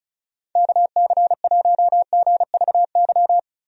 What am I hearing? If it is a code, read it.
Morse code, heard as a keyed tone, about 36 wpm, 700 Hz. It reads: KC1GVY
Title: USA Call Signs - 6 Characters - Repeated 2x - 35wpm